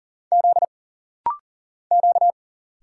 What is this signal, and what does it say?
Morse code, heard as a keyed tone, about 40 wpm, 700 Hz. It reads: Z Q
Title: Single Letters - Rapid-Fire 40wpm